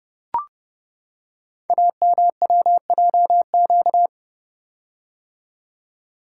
Morse code, heard as a keyed tone, about 29 wpm, 700 Hz. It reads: AMWJQ
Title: Code Groups - 5 Letters - 30wpm